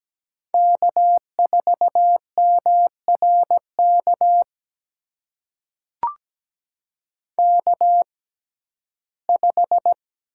Morse code, heard as a keyed tone, about 17 wpm, spaced as for 12 wpm, 700 Hz. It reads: K4MRK K5
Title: USA Call Signs - 5 Characters - 6x Character Spacing - 17wpm